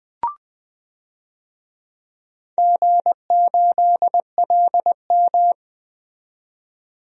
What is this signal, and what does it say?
Morse code, heard as a keyed tone, about 20 wpm, 700 Hz. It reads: G8LM
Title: DXCC-100 Call Signs - 4 Characters - 20wpm